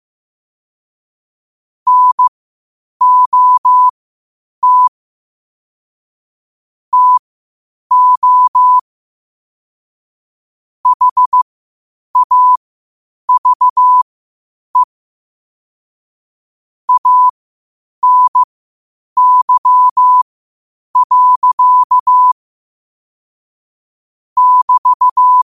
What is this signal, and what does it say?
Morse code, heard as a keyed tone, about 15 wpm, spaced as for 8 wpm, 1000 Hz. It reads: NOT TO HAVE ANY. <BT>
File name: qotdfiles / 8wpm